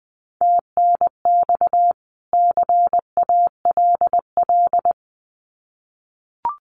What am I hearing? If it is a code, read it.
Morse code, heard as a keyed tone, about 20 wpm, 700 Hz. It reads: TNX CALL
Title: Common QSO Phrases 20wpm